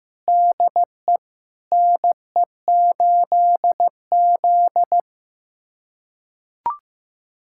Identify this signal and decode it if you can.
Morse code, heard as a keyed tone, about 16 wpm, 700 Hz. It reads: DE NE8Z
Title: USA Callsign DE Callsign 15wpm